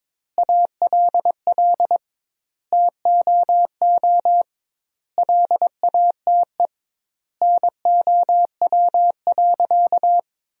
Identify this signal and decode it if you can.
Morse code, heard as a keyed tone, about 22 wpm, 700 Hz. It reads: ALL TOO LATE NOW.
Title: 2x Word-Spacing - Sentences from Top 300 Words - 22wpm